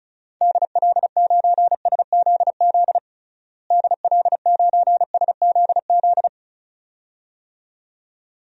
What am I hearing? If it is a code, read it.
Morse code, heard as a keyed tone, about 34 wpm, 700 Hz. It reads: DL9SZZ DL9SZZ
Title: DXCC-100 Call Signs - 6 Characters - Repeated 2x - 35wpm